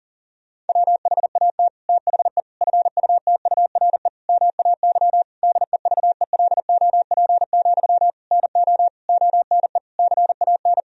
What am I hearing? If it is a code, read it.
Morse code, heard as a keyed tone, about 40 wpm, 700 Hz. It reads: WHAT THE FUTURE MAY DEVELOP, NO ONE CAN
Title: Sentences from Top 1000 Words - Review Part 2 - 40wpm